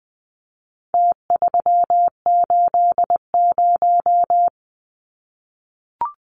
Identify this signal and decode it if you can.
Morse code, heard as a keyed tone, about 20 wpm, 700 Hz. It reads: T380
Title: Numbers - Contest Style - 4-Digit 20wpm